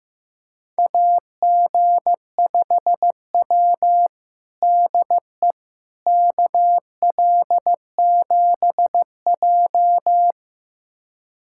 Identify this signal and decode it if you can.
Morse code, heard as a keyed tone, about 15 wpm, 700 Hz. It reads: AG5W DE KL7J
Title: USA Callsign DE Callsign - 4 Characters - 15wpm